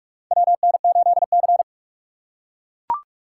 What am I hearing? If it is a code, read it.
Morse code, heard as a keyed tone, about 43 wpm, 700 Hz. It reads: WN8C
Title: USA Call Signs - 4 Characters - 4x Character Spacing - 45wpm